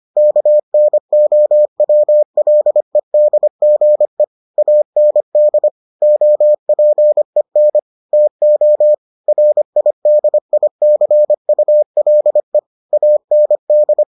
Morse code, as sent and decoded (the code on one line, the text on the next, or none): -.- -. --- .-- .-.. -.. --. / .- -. -.. / --- .--. -. / - --- / .-. .. -.. .. -.-. ..- .-.. / .- -. -..
KNOWLEDGE AND OPEN TO RIDICULE AND